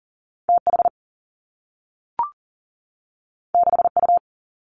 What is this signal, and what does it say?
Morse code, heard as a keyed tone, about 41 wpm, 700 Hz. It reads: TH 6U
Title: Two Challenging Letter-Number Combinations 40wpm